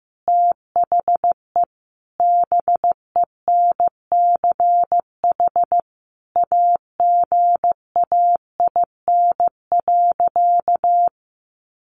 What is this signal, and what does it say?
Morse code, heard as a keyed tone, about 15 wpm, 700 Hz. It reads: THE BENCH AGAIN.
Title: Sentences from Top 1400 Words 15wpm